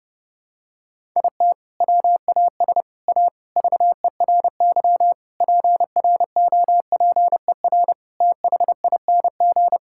Morse code, heard as a keyed tone, about 32 wpm, 700 Hz. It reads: IT WAS A VERY PROPER THING
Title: Sentences from Top 1000 Words - Review Part 5 - 30wpm